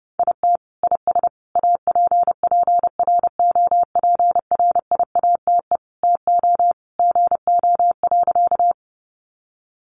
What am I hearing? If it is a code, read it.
Morse code, heard as a keyed tone, about 30 wpm, 700 Hz. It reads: IT IS APPROPRIATE TO GO.
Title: Sentences from Top 900 Words 30wpm